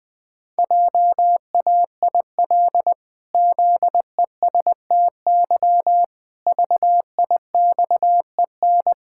Morse code, heard as a keyed tone, about 20 wpm, 700 Hz. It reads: JAIL ZESTY VIXEN
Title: Pangram Challenge v2 - 20wpm